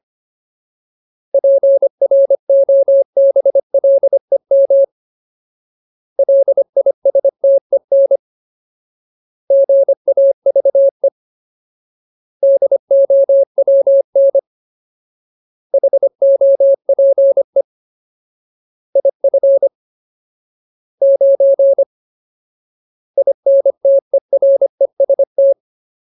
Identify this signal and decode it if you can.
Morse code, heard as a keyed tone, about 25 wpm, 550 Hz. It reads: PROBLEM LISTEN GAVE DOWN HOPE IF 9 INTEREST
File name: Ultimate Word List / 25WPM Ultimate